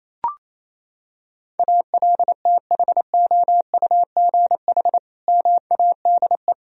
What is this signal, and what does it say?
Morse code, heard as a keyed tone, about 28 wpm, 700 Hz. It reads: ALTHOUGH MADE
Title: Sentences from Top 1000 Words - Review Part 3 - 28wpm